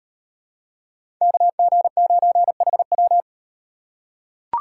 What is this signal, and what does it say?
Morse code, heard as a keyed tone, about 36 wpm, 700 Hz. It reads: KG9HW